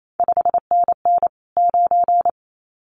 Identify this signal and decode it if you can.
Morse code, heard as a keyed tone, about 28 wpm, 700 Hz. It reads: 5NN 9
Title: CQ WW DX Contest Exchange - 28wpm